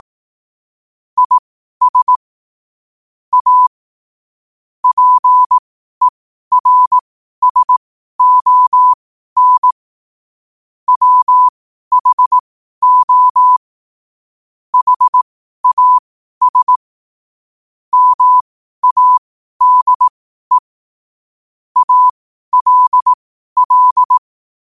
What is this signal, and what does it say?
Morse code, heard as a keyed tone, about 18 wpm, spaced as for 12 wpm, 1000 Hz. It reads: IS A PERSON WHO HAS MADE ALL